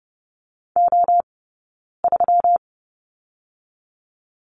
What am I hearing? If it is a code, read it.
Morse code, heard as a keyed tone, about 30 wpm, 700 Hz. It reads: O 3
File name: Lesson 20d - Number 3 - Two Characters 30wpm